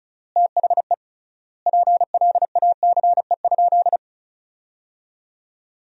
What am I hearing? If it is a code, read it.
Morse code, heard as a keyed tone, about 34 wpm, 700 Hz. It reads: THE PLACE?
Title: 3x Word-Spacing - Sentences from Top 200 Words - 35wpm